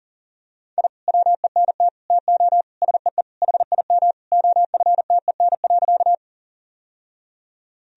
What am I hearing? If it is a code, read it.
Morse code, heard as a keyed tone, about 37 wpm, 700 Hz. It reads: I WENT TO SEE HIM OFTEN.
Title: Sentences from Top 1000 Words - Review Part 1 - 40wpm